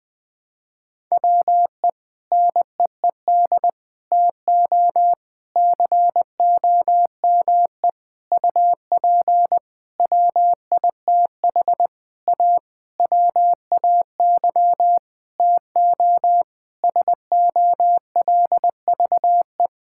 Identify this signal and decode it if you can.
Morse code, heard as a keyed tone, about 20 wpm, 700 Hz. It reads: WE NEED TO COME UP WITH A WAY TO SOLVE
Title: Sentences with 7-Word Phrases - Spoken Beginning - 20wpm